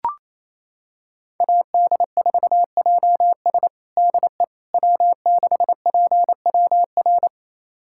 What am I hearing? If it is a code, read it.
Morse code, heard as a keyed tone, about 28 wpm, 700 Hz. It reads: AD4JS DE W6PWR
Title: USA Callsign DE Callsign - 5 Characters - 28wpm